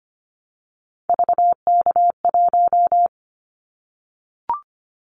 Morse code, heard as a keyed tone, about 25 wpm, 700 Hz. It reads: VK1
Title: Callsign Prefixes for Top 100 DXCC - 25wpm